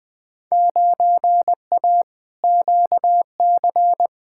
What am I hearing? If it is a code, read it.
Morse code, heard as a keyed tone, about 20 wpm, 700 Hz. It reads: 9A QC
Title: Field Day Exchanges Encore - 20wpm